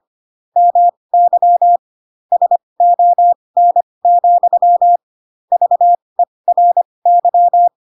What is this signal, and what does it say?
Morse code, heard as a keyed tone, about 25 wpm, spaced as for 20 wpm, 700 Hz. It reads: MY SON, VERY
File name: Pinocchio / Pinocchio ch3 15 25 ik1hge